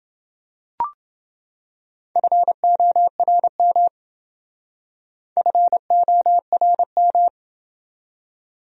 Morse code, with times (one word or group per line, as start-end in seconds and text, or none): 2.2-3.9 FORM
5.4-7.3 FORM